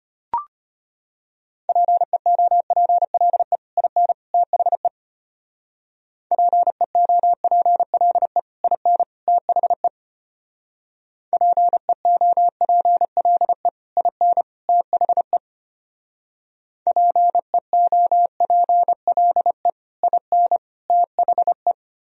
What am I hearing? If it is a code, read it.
Morse code, heard as a keyed tone, about 32 wpm, 700 Hz. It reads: PEOPLE IN THE PEOPLE IN THE PEOPLE IN THE PEOPLE IN THE